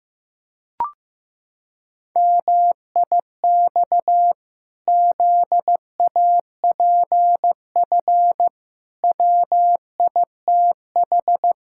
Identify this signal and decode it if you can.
Morse code, heard as a keyed tone, about 16 wpm, 700 Hz. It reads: MIX ZAPF WITH
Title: Pangram Challenge v2 - 15wpm